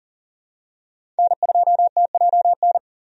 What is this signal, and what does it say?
Morse code, heard as a keyed tone, about 40 wpm, 700 Hz. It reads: N2TJN
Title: USA Call Signs - 5 Characters - 40wpm